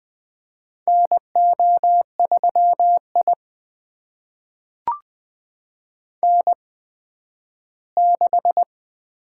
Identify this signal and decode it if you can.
Morse code, heard as a keyed tone, about 20 wpm, spaced as for 12 wpm, 700 Hz. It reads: NO3I N6
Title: USA Call Signs - 4 Characters - 8x Character Spacing - 20wpm